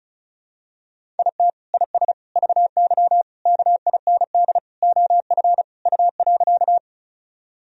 Morse code, heard as a keyed tone, about 36 wpm, 700 Hz. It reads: IT IS VY KIND OF U.
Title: Sentences from Top 200 Words and Abbreviations 35wpm